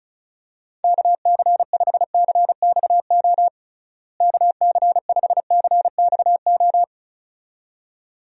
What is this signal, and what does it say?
Morse code, heard as a keyed tone, about 34 wpm, 700 Hz. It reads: KC5CXO KC5CXO
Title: USA Call Signs - 6 Characters - Repeated 2x - 35wpm